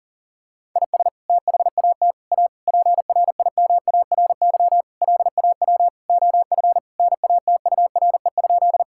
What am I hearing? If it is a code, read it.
Morse code, heard as a keyed tone, about 40 wpm, 700 Hz. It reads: IS THAT A PRIMARY LAW OF NATURE?
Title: Sentences from Top 1000 Words - Review Part 5 - 40wpm